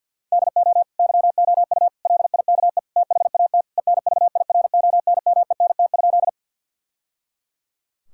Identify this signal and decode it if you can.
Morse code, heard as a keyed tone, about 51 wpm, 700 Hz. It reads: DO YOU LIKE THAT ENVIRONMENT?
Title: Sentences from Top 1000 Words - Review Part 5 - 50wpm